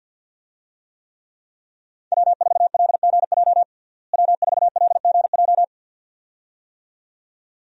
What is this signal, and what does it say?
Morse code, heard as a keyed tone, about 46 wpm, 700 Hz. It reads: W4LGJ W4LGJ
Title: USA Call Signs - 5 Characters - Repeated 2x - 50wpm